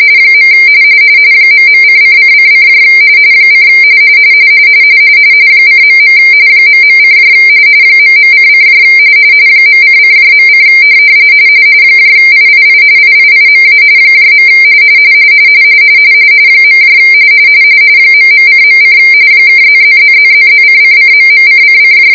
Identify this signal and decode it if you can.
RTTY, about 45.45 baud. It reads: . ..::::: : :.. ..: ..:: .::.: III IIIMA . ..:.: (++++++:::..:::